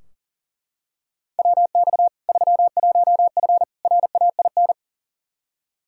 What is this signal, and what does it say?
Morse code, heard as a keyed tone, about 40 wpm, 700 Hz. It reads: WX 31F RAIN
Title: Practice QSO - Exchange 2 - Info - 40wpm